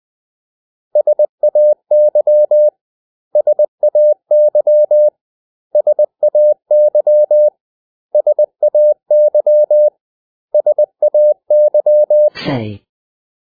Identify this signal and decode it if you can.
Morse code, heard as a keyed tone, about 20 wpm, 600 Hz. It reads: SAY SAY SAY SAY SAY
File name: cw / 100 Common English Words